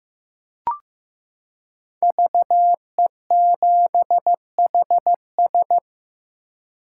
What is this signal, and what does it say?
Morse code, heard as a keyed tone, about 14 wpm, 700 Hz. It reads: VE7HS
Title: K1USN SST QSO Practice - 15wpm